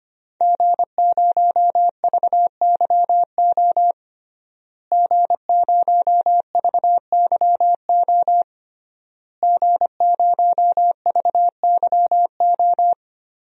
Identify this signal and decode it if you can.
Morse code, heard as a keyed tone, about 25 wpm, 700 Hz. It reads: G0VYO G0VYO G0VYO
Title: DXCC-100 Call Signs - 5 Characters - Repeated 3x - 25wpm